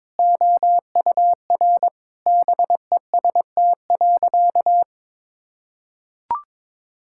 Morse code, heard as a keyed tone, about 22 wpm, 700 Hz. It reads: OUR BEST.
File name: Sentences with 7-Word Phrases - 22wpm